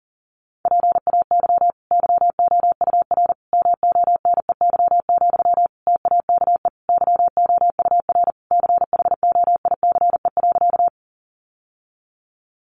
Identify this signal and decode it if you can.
Morse code, heard as a keyed tone, about 40 wpm, 700 Hz. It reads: PAY YOUR MONEY, TAKE YOUR CHOICE.
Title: Sentences from Top 900 Words 40wpm